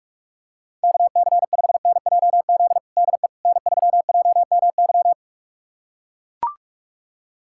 Morse code, heard as a keyed tone, about 41 wpm, 700 Hz. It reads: KC5NJZ DE N3JMY
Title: USA Callsign DE Callsign 45wpm